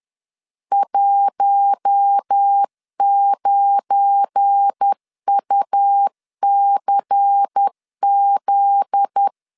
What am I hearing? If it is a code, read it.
Morse code, heard as a keyed tone, about 10 wpm, 800 Hz. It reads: 19UCZ
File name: CW / DET / 1 9 1500C S A 10wpm 4511